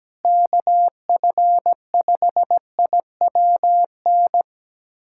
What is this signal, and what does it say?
Morse code, heard as a keyed tone, about 17 wpm, 700 Hz. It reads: KF5IWN
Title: USA Callsign DE Callsign - 6 Characters - 17wpm